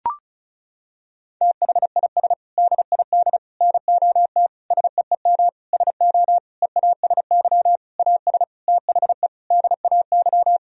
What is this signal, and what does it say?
Morse code, heard as a keyed tone, about 35 wpm, 700 Hz. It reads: THIS DID NOT SEEM SO EASY AS THE DAY
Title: Sentences from Top 1000 Words - Review Part 1 - 35wpm